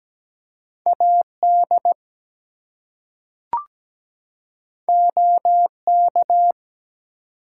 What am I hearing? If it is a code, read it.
Morse code, heard as a keyed tone, about 17 wpm, 700 Hz. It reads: AD OK
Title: Two Letter Words - 17wpm